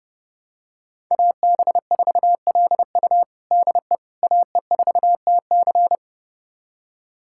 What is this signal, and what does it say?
Morse code, heard as a keyed tone, about 30 wpm, 700 Hz. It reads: AB4LU DE AE4TC
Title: USA Callsign DE Callsign - 5 Characters - 30wpm